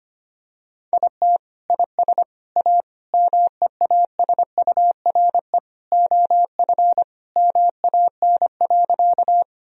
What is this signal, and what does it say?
Morse code, heard as a keyed tone, about 25 wpm, 700 Hz. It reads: IT IS A MEASURE OF MAN.
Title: Sentences from Top 1000 Words - Review Part 2 - 25wpm